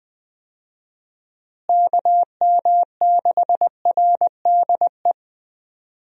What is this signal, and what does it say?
Morse code, heard as a keyed tone, about 20 wpm, 700 Hz. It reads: KM6RDE
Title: USA Call Signs - 6 Characters - Repeated 2x - 20wpm